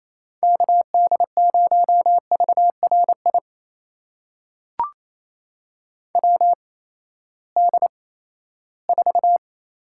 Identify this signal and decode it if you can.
Morse code, heard as a keyed tone, about 27 wpm, spaced as for 16 wpm, 700 Hz. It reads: KD0VRI WD4
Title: USA Call Signs - 6 Characters - 8x Character Spacing - 28wpm